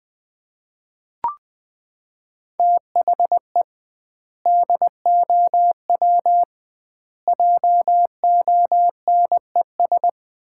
Morse code, heard as a keyed tone, about 20 wpm, 700 Hz. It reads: THE DOW JONES
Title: 2x Word-Spacing - Sets of 5 Words - 20wpm